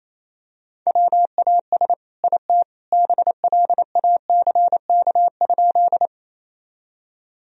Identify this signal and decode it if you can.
Morse code, heard as a keyed tone, about 28 wpm, 700 Hz. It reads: WAS IT BLACK?
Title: Sentences from Top 1000 Words - Review Part 2 - 28wpm